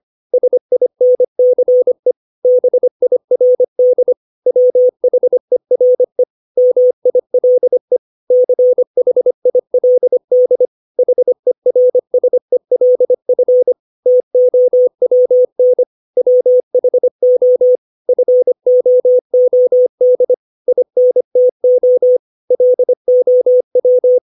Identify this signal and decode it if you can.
Morse code, heard as a keyed tone, about 25 wpm, 500 Hz. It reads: SINCE BIRD WHERE MILE CHILD HERSELF TOWN WHO FOOD INTO LOW
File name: 632 words / 632 Words 25WPM